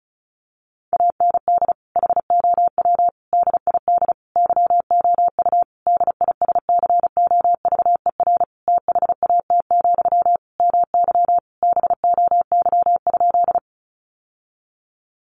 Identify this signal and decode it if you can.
Morse code, heard as a keyed tone, about 34 wpm, 700 Hz. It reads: AND HOW DID YOU DISCOVER THAT, MY BOY?
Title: Sentences from Top 1100 Words 35wpm